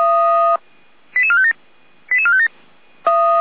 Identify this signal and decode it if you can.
Five-tone selective call (ZVEI), heard as quick runs of five tones, tones of about 70 ms. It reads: 88457 88457